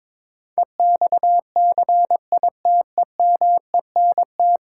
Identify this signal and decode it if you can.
Morse code, heard as a keyed tone, about 22 wpm, 700 Hz. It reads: EXCITEMENT
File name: Difficult Sentences with Hints - 22wpm